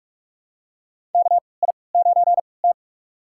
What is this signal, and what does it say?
Morse code, heard as a keyed tone, about 41 wpm, spaced as for 25 wpm, 690 Hz. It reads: KI9T
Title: USA Call Signs - 4 Characters - 3x Character Spacing - 45wpm